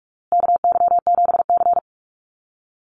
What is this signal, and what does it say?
Morse code, heard as a keyed tone, about 44 wpm, 700 Hz. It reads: KY7C
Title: USA Call Signs Repeated 3x 45wpm